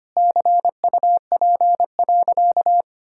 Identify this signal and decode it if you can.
Morse code, heard as a keyed tone, about 25 wpm, 700 Hz. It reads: CUP.
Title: Sentences from Top 1600 Words 25wpm